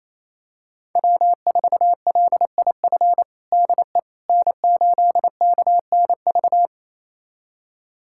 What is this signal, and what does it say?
Morse code, heard as a keyed tone, about 28 wpm, 700 Hz. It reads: W4LIF DE N8KNV
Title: USA Callsign DE Callsign - 5 Characters - 28wpm